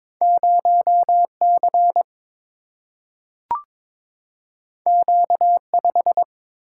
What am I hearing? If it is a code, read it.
Morse code, heard as a keyed tone, about 22 wpm, 700 Hz. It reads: 0C Q5
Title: Single-Letter Single-Number 22wpm